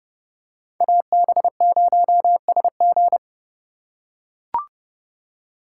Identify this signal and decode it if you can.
Morse code, heard as a keyed tone, about 30 wpm, 700 Hz. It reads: AB0SG
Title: USA Call Signs - 2x Character Spacing - 30wpm